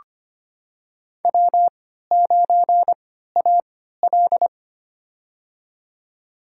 Morse code, heard as a keyed tone, about 25 wpm, spaced as for 14 wpm, 700 Hz. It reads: W9AL
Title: USA Call Signs - 3x Character Spacing - 25wpm